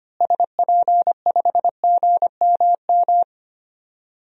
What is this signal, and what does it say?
Morse code, heard as a keyed tone, about 25 wpm, 700 Hz. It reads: SP5GMM
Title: DXCC-100 Call Signs - 6 Characters - Repeated 2x - 25wpm